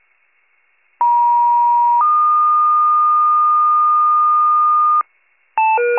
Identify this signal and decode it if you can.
Two-tone page, heard as a one-second tone, then a three-second tone, about 950 then 1220 Hz, with noise. It is heard from 1.0 to 5.0 s.